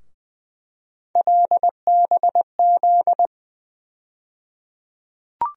Morse code, heard as a keyed tone, about 20 wpm, 700 Hz. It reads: LBZ
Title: USA Call Sign Postfixes - 3 Letters 20wpm